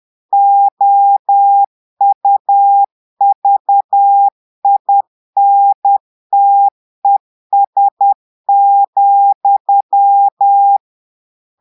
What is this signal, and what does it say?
Morse code, heard as a keyed tone, about 10 wpm, 800 Hz. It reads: OUVINTES,
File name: portugais texte en clair 2 vitesse 10 mots minutes